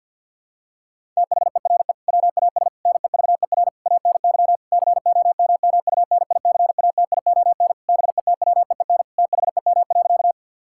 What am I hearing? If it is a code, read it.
Morse code, heard as a keyed tone, about 50 wpm, 700 Hz. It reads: THERE WAS NEVER ANY COMMUNICATION BETWEEN THEM.
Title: Sentences from Top 1000 Words - Review Part 6 - 50wpm